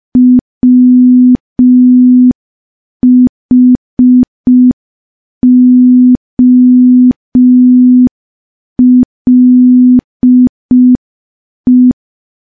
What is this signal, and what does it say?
Morse code, heard as a keyed tone, about 5 wpm, 250 Hz. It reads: WHOLE
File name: morse / morse-training-05WPM